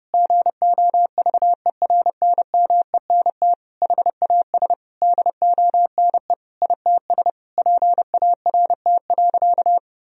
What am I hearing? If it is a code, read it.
Morse code, heard as a keyed tone, about 32 wpm, 700 Hz. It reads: GOVERNMENT HAS DONE ITS PART.
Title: Sentences from Top 1000 Words - Review Part 3 - 30wpm